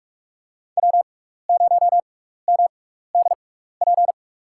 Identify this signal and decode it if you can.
Morse code, heard as a keyed tone, about 47 wpm, spaced as for 16 wpm, 690 Hz. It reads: W0MDP
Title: USA Call Signs - 5 Characters - 6x Character Spacing - 45wpm